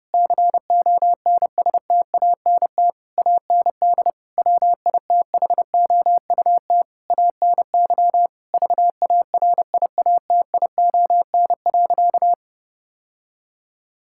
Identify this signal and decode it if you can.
Morse code, heard as a keyed tone, about 30 wpm, 700 Hz. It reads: CONSTANT AND WITHOUT ANY VARIATION.
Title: Sentences from Top 1600 Words 30wpm